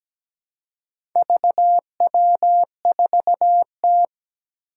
Morse code, heard as a keyed tone, about 17 wpm, 700 Hz. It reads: VW4T
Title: Four Challenging Letter-Number Combinations 17wpm